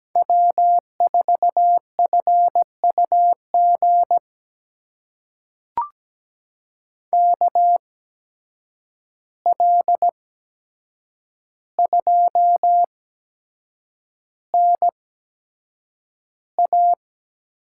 Morse code, heard as a keyed tone, about 17 wpm, spaced as for 7 wpm, 700 Hz. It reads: W4FUG KL2NA
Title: USA Call Signs - 8x Character Spacing - 17wpm